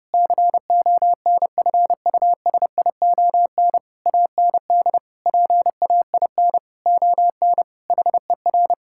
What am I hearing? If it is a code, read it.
Morse code, heard as a keyed tone, about 30 wpm, 700 Hz. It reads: CONFUSION AND PAIN ON HER
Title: Sentences from Top 1600 Words 30wpm